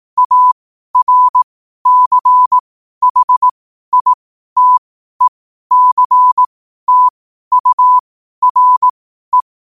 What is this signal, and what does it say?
Morse code, heard as a keyed tone, about 18 wpm, spaced as for 12 wpm, 1000 Hz. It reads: ARCHITECTURE
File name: qotdfiles / 12wpm